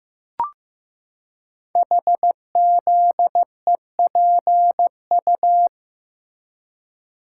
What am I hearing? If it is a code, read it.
Morse code, heard as a keyed tone, about 15 wpm, 700 Hz. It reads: HZEPU
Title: Code Groups - 5 Letters - 15wpm